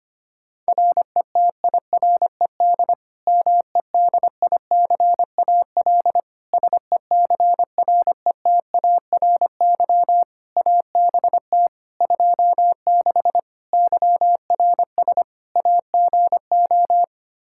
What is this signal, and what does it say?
Morse code, heard as a keyed tone, about 25 wpm, 700 Hz. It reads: RETIRED MEDICAL SECRETARY ABT 26 YRS AGO
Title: K1USN SST QSO Practice 25wpm